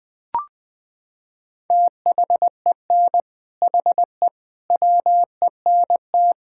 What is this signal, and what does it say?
Morse code, heard as a keyed tone, about 21 wpm, 700 Hz. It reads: THEN HE WENT
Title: Sentences from Top 1000 Words - Review Part 5 - 20wpm